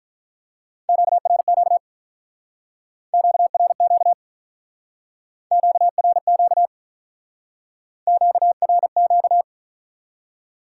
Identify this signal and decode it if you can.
Morse code, heard as a keyed tone, about 45 wpm, 700 Hz. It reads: QRQ QRQ QRQ QRQ